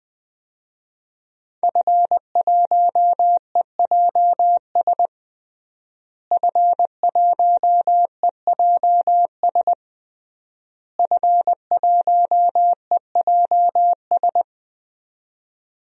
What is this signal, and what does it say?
Morse code, heard as a keyed tone, about 20 wpm, 700 Hz. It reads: F1EJS F1EJS F1EJS
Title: DXCC-100 Call Signs - 5 Characters - Repeated 3x - 20wpm